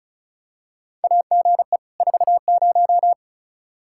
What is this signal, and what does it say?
Morse code, heard as a keyed tone, about 34 wpm, 700 Hz. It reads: AGE 40
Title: K1USN SST QSO Practice 35wpm